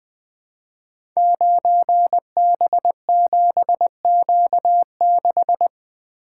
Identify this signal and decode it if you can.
Morse code, heard as a keyed tone, about 20 wpm, 700 Hz. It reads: 9B7Q6
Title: Code Groups - 5 Letter:Numbers - 20wpm